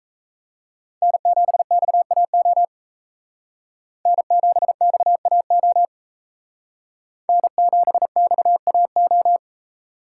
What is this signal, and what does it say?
Morse code, heard as a keyed tone, about 34 wpm, 700 Hz. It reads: N7XAO N7XAO N7XAO
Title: Speed Racing - USA 1x3 Call Signs 28wpm